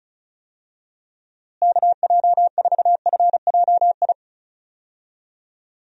Morse code, heard as a keyed tone, about 36 wpm, 700 Hz. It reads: KJ4FJI
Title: USA Call Signs - 6 Characters - Repeated 2x - 35wpm